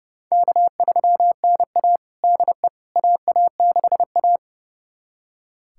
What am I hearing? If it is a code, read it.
Morse code, heard as a keyed tone, about 29 wpm, 700 Hz. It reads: K3NA DE AA6A
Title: USA Callsign DE Callsign - 4 Characters - 30wpm